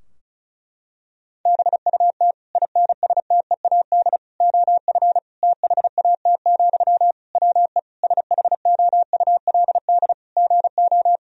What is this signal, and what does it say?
Morse code, heard as a keyed tone, about 36 wpm, 700 Hz. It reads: BUT INSTEAD OF THAT, WE SHOULD GO
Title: Sentences from Top 1000 Words - Review Part 4 - 35wpm